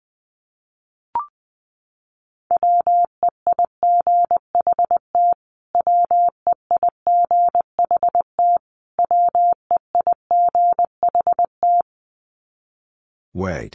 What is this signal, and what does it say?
Morse code, heard as a keyed tone, about 20 wpm, 700 Hz. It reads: WEIGHT WEIGHT WEIGHT
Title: Top 500 Words Repeated 20wpm